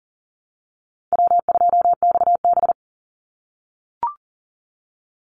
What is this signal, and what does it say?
Morse code, heard as a keyed tone, about 42 wpm, 700 Hz. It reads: W2XB
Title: Easy Call Signs - 40wpm